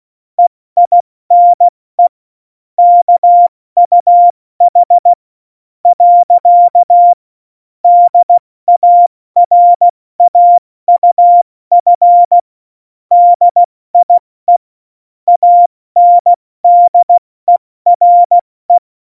Morse code, heard as a keyed tone, about 16 wpm, spaced as for 14 wpm, 700 Hz. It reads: EINE KUH. DARAUF DIE ANDERE
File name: Morsekurs / Witze / MP3, Witz 08. Mai 2018, 70 F80